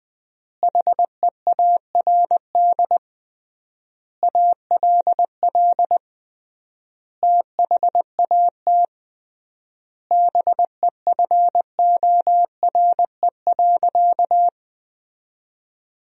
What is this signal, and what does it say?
Morse code, heard as a keyed tone, about 20 wpm, 700 Hz. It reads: HEARD ALL THAT BEFORE.
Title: 3x Word-Spacing - Sentences from Top 400 Words - 20wpm